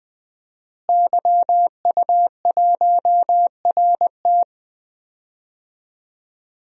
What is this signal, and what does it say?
Morse code, heard as a keyed tone, about 20 wpm, 700 Hz. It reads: YU1RT
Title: DXCC-100 Call Signs - 5 Characters - Repeated 3x - 20wpm